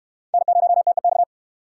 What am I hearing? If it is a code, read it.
Morse code, heard as a keyed tone, about 64 wpm, 700 Hz. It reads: N0TEX